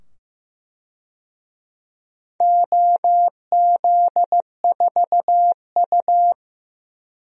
Morse code, heard as a keyed tone, about 15 wpm, 700 Hz. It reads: OZ4U
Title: DXCC-100 Call Signs - 4 Characters - 15wpm